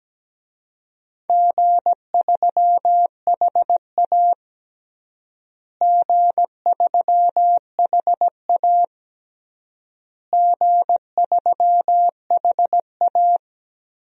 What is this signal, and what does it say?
Morse code, heard as a keyed tone, about 17 wpm, 700 Hz. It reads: G3HA G3HA G3HA
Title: DXCC-100 Call Signs - 4 Characters - Repeated 3x - 17wpm